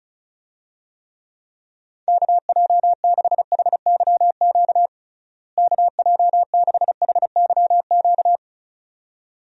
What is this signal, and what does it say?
Morse code, heard as a keyed tone, about 34 wpm, 700 Hz. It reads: KJ6HYQ KJ6HYQ
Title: USA Call Signs - 6 Characters - Repeated 2x - 35wpm